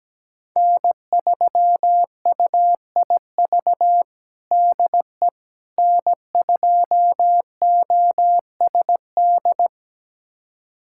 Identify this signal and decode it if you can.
Morse code, heard as a keyed tone, about 17 wpm, 700 Hz. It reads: N3UIV DE N2OSD
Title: USA Callsign DE Callsign - 5 Characters - 17wpm